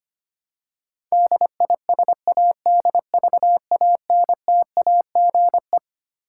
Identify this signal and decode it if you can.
Morse code, heard as a keyed tone, about 25 wpm, 700 Hz. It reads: DISADVANTAGE
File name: Twelve Letter Words - 25wpm